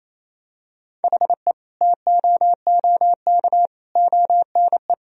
Morse code, heard as a keyed tone, about 28 wpm, 700 Hz. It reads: HE TOOK ONE
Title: Sentences from Top 1000 Words - Review Part 1 - 28wpm